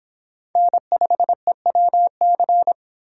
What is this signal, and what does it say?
Morse code, heard as a keyed tone, about 26 wpm, 700 Hz. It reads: N5EWC